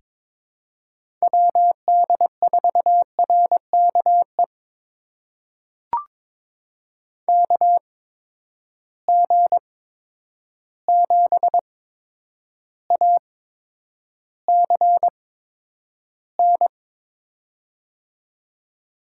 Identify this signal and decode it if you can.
Morse code, heard as a keyed tone, about 22 wpm, spaced as for 9 wpm, 700 Hz. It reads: WD4RKE KG7ACN
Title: USA Call Signs - 6 Characters - 8x Character Spacing - 22wpm